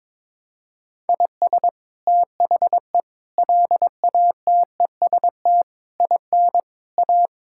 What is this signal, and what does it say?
Morse code, heard as a keyed tone, about 22 wpm, 700 Hz. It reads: IS THE LATEST IN A
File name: practice / 22wpm / Sets of 7 Words - 22wpm